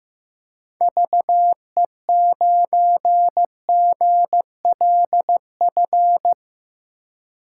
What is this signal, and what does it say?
Morse code, heard as a keyed tone, about 15 wpm, 700 Hz. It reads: VE9GLF
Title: K1USN SST QSO Practice - 15wpm